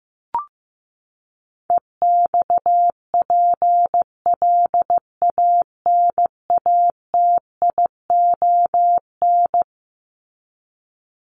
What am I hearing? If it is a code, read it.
Morse code, heard as a keyed tone, about 15 wpm, 700 Hz. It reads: EXPLANATION
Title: Top 1751-2000 Words 15wpm